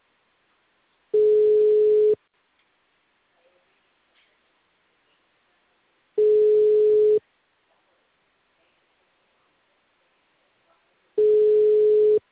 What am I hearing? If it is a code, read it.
Ringback tone (Europe, 425 Hz).
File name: out- -2012- - - .76565